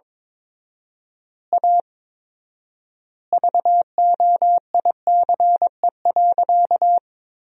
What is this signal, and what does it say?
Morse code, heard as a keyed tone, about 22 wpm, 700 Hz. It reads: A VOICE.
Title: 4x Word-Spacing - Sentences from Top 500 Words - 22wpm